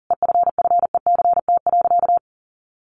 Morse code, heard as a keyed tone, about 40 wpm, 700 Hz. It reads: EFFECT.
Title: Sentences from Top 1400 Words 40wpm